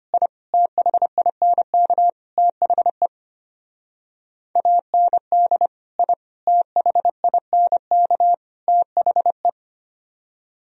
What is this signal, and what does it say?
Morse code, heard as a keyed tone, about 33 wpm, 700 Hz. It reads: I THINK THE AND I THINK THE